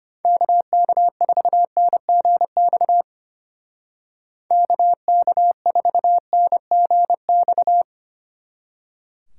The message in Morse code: KK4NGX KK4NGX

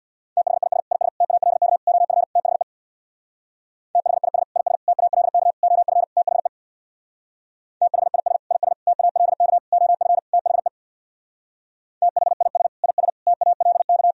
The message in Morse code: THIS IS TALK OF THE THIS IS TALK OF THE THIS IS TALK OF THE THIS IS TALK

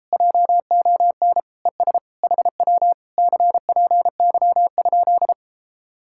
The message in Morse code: JON ES HW CPY?

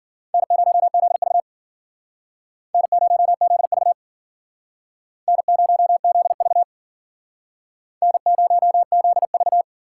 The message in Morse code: N0ZV N0ZV N0ZV N0ZV

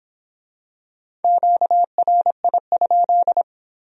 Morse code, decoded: QRI?

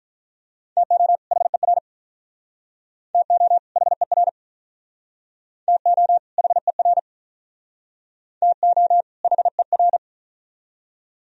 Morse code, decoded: TO HER TO HER TO HER TO HER

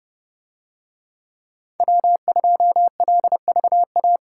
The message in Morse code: W2LVA

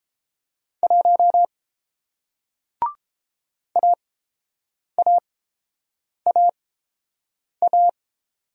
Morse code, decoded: 1 AAAA

